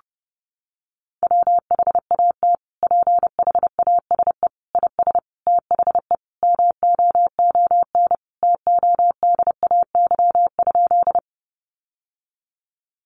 Morse code, heard as a keyed tone, about 30 wpm, 700 Hz. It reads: WHAT PHASE IS THE MOON TODAY?
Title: Sentences from Top 1200 Words 30wpm